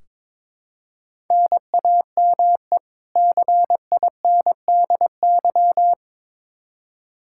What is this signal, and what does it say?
Morse code, heard as a keyed tone, about 22 wpm, 700 Hz. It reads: NAME CINDY
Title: K1USN SST QSO Practice 22wpm